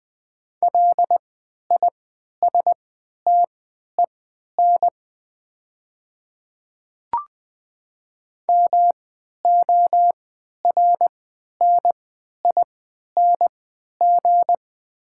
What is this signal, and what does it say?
Morse code, heard as a keyed tone, about 21 wpm, spaced as for 12 wpm, 700 Hz. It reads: LISTEN MORNING
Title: 3x Character Spacing - Top 500 Words - 20wpm